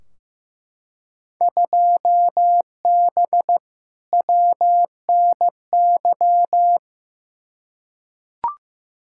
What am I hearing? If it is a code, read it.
Morse code, heard as a keyed tone, about 15 wpm, 700 Hz. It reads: 2B WNY
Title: Field Day Exchanges Encore - 15wpm